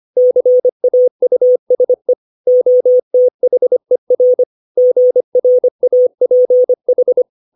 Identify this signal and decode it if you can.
Morse code, heard as a keyed tone, about 25 wpm, 500 Hz. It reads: CAUSE OTHER GRAPH
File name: 632 words / 632 Words 25WPM